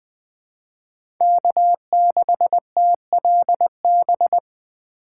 Morse code, heard as a keyed tone, about 20 wpm, 700 Hz. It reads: K6TLB
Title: USA Call Signs - 2x Character Spacing - 20wpm